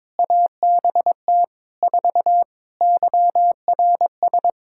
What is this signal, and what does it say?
Morse code, heard as a keyed tone, about 22 wpm, 700 Hz. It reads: ABT 4 YRS
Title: Practice QSO - Exchange 3 - More Info - 22wpm